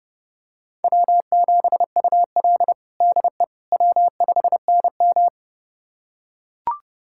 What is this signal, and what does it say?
Morse code, heard as a keyed tone, about 31 wpm, 700 Hz. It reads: W7UL DE W5NM
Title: USA Callsign DE Callsign - 4 Characters - 30wpm